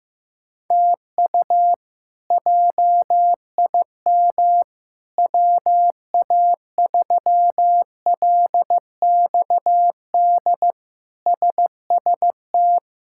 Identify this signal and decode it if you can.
Morse code, heard as a keyed tone, about 15 wpm, 700 Hz. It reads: TU JIM WA3LXD SST
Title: K1USN SST QSO Practice - 15wpm